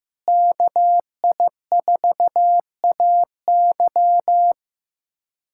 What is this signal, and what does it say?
Morse code, heard as a keyed tone, about 15 wpm, 700 Hz. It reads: KI4AY
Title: USA Callsign DE Callsign - 5 Characters - 15wpm